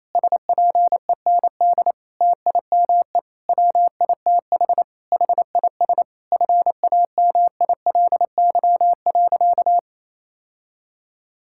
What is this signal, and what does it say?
Morse code, heard as a keyed tone, about 28 wpm, 700 Hz. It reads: SPEND TIME WITH HIS FAMILY.
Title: Sentences from Top 1000 Words - Review Part 6 - 28wpm